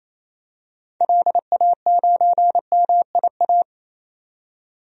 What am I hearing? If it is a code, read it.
Morse code, heard as a keyed tone, about 28 wpm, 700 Hz. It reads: LA9MIA
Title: DXCC-100 Call Signs - 6 Characters - 28wpm